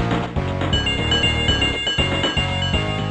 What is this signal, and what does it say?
There is no code.